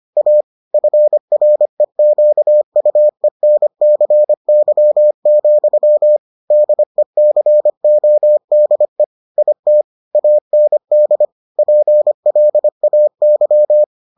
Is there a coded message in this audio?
Yes: Morse code from 0.2 to 13.8 s, heard as a keyed tone, about 25 wpm, 600 Hz. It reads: A FREQUENCY, DECODE IT AND PLAY